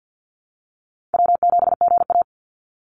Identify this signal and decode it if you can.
Morse code, heard as a keyed tone, about 50 wpm, 700 Hz. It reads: W7GA
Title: NA SOTA Chaser Callsigns 50wpm